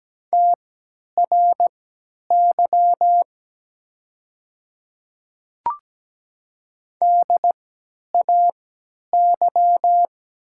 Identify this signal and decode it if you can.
Morse code, heard as a keyed tone, about 17 wpm, spaced as for 10 wpm, 700 Hz. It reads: TRY DAY
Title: 3x Character Spacing - Top 200 Words - 17wpm